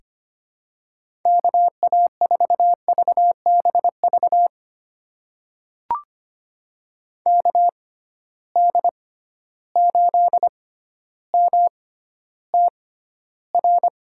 Morse code, heard as a keyed tone, about 25 wpm, spaced as for 13 wpm, 700 Hz. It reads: KA4VBV KD8MTR
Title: USA Call Signs - 6 Characters - 6x Character Spacing - 25wpm